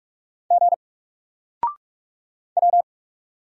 Morse code, heard as a keyed tone, about 46 wpm, 700 Hz. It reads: G W
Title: Single Letter-Number Alt 45wpm